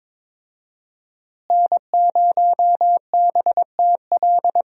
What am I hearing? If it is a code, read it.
Morse code, heard as a keyed tone, about 22 wpm, 700 Hz. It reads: N0BTL
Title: USA Call Signs - 5 Characters - Repeated 3x - 22wpm